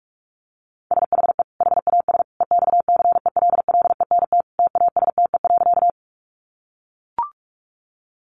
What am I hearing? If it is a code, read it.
Morse code, heard as a keyed tone, about 44 wpm, 700 Hz. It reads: SHE HAS EXCELLENT TASTE.
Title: Sentences from Top 1100 Words 45wpm